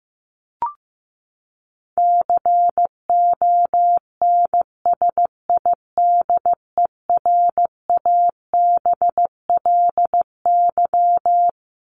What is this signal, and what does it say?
Morse code, heard as a keyed tone, about 15 wpm, 700 Hz. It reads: CONSIDERABLY
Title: LY Suffix - Top 100 Words with Base Words 15wpm